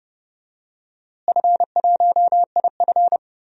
Morse code, heard as a keyed tone, about 31 wpm, 700 Hz. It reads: F1IF
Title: DXCC-100 Call Signs - 4 Characters - Repeated 3x - 30wpm